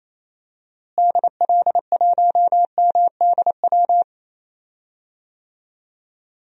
Morse code, heard as a keyed tone, about 27 wpm, 700 Hz. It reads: DL1MDW
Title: DXCC-100 Call Signs - 6 Characters - Repeated 2x - 28wpm